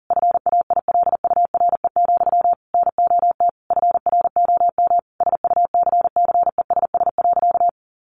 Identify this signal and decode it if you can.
Morse code, heard as a keyed tone, about 40 wpm, 700 Hz. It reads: FAILURE, NOT FROM SUCCESS.
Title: Sentences from Top 900 Words 40wpm